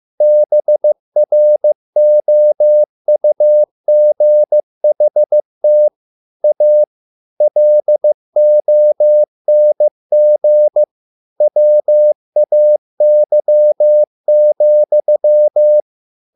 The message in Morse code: BROUGHT A LONG WAY,

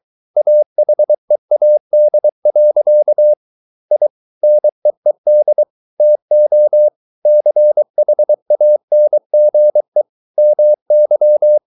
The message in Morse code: AHEAD. I NEED TO CHANGE MY

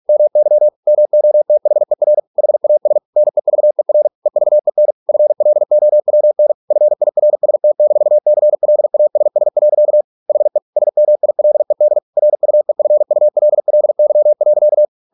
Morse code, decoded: MY MOTHER HAS NEVER EVEN FLOWN FIRST-CLASS. HE SMILED RUEFULLY.

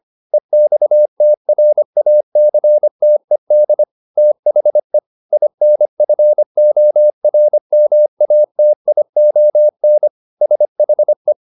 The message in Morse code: EXTRACTED THE INFORMATION SHE

